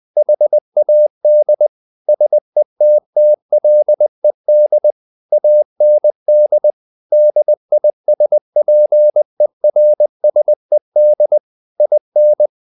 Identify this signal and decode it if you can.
Morse code, heard as a keyed tone, about 20 wpm, 600 Hz. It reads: HAD SETTLED AND DISPERSED IN
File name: the old man and the sea part 2 20wpm